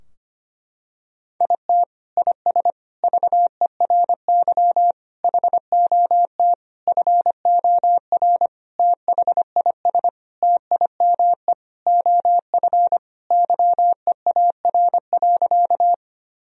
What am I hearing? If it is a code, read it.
Morse code, heard as a keyed tone, about 25 wpm, 700 Hz. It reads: IT IS VERY HOT FOR THIS TIME OF YEAR.
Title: Sentences from Top 1000 Words - Review Part 3 - 25wpm